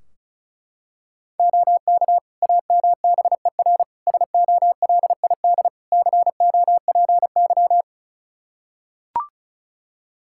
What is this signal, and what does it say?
Morse code, heard as a keyed tone, about 36 wpm, 700 Hz. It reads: OK AMBER SOLID COPY
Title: Practice QSO - Exchange 3 - More Info - 35wpm